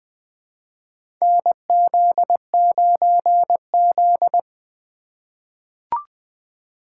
Morse code, heard as a keyed tone, about 21 wpm, 700 Hz. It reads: NZ9Z
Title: USA Call Signs - 8x Character Spacing - 20wpm